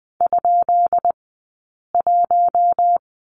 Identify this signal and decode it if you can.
Morse code, heard as a keyed tone, about 19 wpm, 700 Hz. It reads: ? 1
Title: 177 Lesson 29d - Slash - Two Characters 20wpm